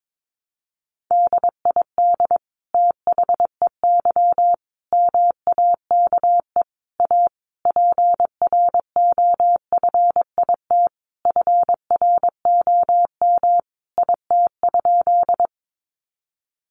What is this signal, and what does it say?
Morse code, heard as a keyed tone, about 22 wpm, 700 Hz. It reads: DID THEY MAKE A PROFIT FROM IT?
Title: Sentences from Top 800 Words 22wpm